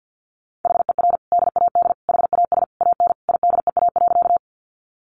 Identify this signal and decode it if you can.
Morse code, heard as a keyed tone, about 50 wpm, 700 Hz. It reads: HER DAD HAS AN IDEA.
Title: Additional Easy Phrases 50wpm